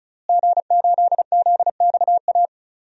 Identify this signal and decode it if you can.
Morse code, heard as a keyed tone, about 34 wpm, 700 Hz. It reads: G8ZXA
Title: DXCC-100 Call Signs - 5 Characters - Repeated 3x - 35wpm